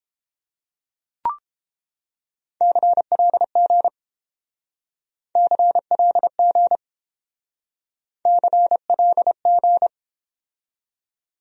Morse code, heard as a keyed tone, about 29 wpm, 700 Hz. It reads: CLG CLG CLG